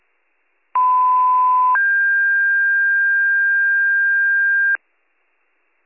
Two-tone page, heard as a one-second tone, then a three-second tone, about 1010 then 1600 Hz, with noise. It is heard from 0.7 to 4.8 s.